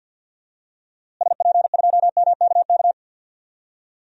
Morse code, heard as a keyed tone, about 52 wpm, 700 Hz. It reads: SP2MKK
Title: DXCC-100 Call Signs - 6 Characters - Repeated 3x - 50wpm